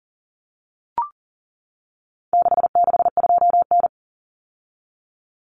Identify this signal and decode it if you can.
Morse code, heard as a keyed tone, about 40 wpm, 700 Hz. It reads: 662N
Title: Common Cut Numbers - 4-Digit 40wpm